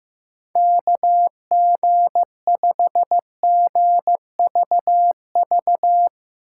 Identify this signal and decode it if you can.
Morse code, heard as a keyed tone, about 16 wpm, 700 Hz. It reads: KG5GVV
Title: Speed Racing - USA 2x3 Call Signs 15wpm